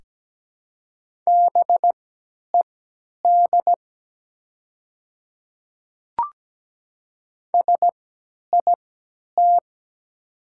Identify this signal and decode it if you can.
Morse code, heard as a keyed tone, about 17 wpm, spaced as for 10 wpm, 700 Hz. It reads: BED SIT
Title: 3x Character Spacing - Top 500 Words - 17wpm